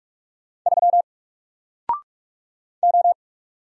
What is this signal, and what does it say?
Morse code, heard as a keyed tone, about 44 wpm, 700 Hz. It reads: 3 O
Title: Single Character (Letter:Number:Punctuation:Prosign) Alt 45wpm